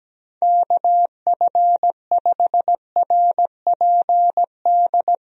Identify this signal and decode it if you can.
Morse code, heard as a keyed tone, about 17 wpm, 700 Hz. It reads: KF5RPD
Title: USA Call Signs - 6 Characters - Repeated 3x - 17wpm